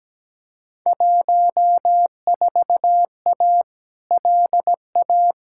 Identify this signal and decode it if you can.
Morse code, heard as a keyed tone, about 17 wpm, 700 Hz. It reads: 14A LA
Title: Field Day Exchanges Encore - 17wpm